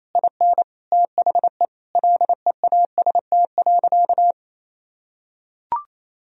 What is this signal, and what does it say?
Morse code, heard as a keyed tone, about 27 wpm, 700 Hz. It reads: IN THE LEAST.
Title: Sentences from Top 1000 Words - Review Part 4 - 28wpm